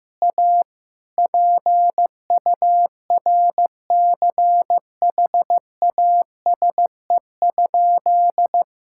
Morse code, heard as a keyed tone, about 15 wpm, 700 Hz. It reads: A PURCHASE?
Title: Sentences from Top 1000 Words - Review Part 6 - 15wpm